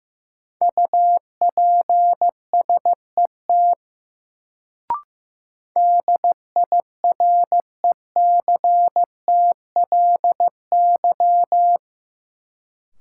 Morse code, heard as a keyed tone, about 15 wpm, 700 Hz. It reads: UPSET DIRECTLY
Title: Sentences from Top 800 Words - New Word Drill - 15wpm